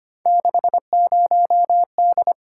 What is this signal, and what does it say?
Morse code, heard as a keyed tone, about 25 wpm, 700 Hz. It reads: 60D
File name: Three Letter-Number Combinations - 25wpm